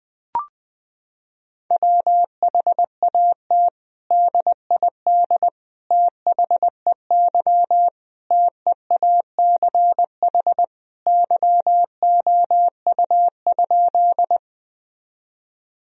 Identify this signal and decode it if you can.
Morse code, heard as a keyed tone, about 20 wpm, 700 Hz. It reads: WHAT DID THEY TEACH YOU?
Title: Sentences from Top 1000 Words - Review Part 2 - 20wpm